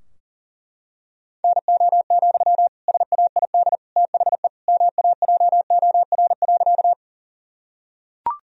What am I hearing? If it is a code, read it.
Morse code, heard as a keyed tone, about 40 wpm, 700 Hz. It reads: NO, SAID THE MAJOR.
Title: Sentences from Top 1000 Words - Review Part 4 - 40wpm